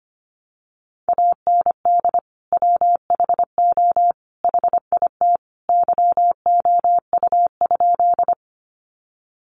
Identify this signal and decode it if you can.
Morse code, heard as a keyed tone, about 25 wpm, 700 Hz. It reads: AND WHO HIT YOU?
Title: Sentences from Top 1200 Words 25wpm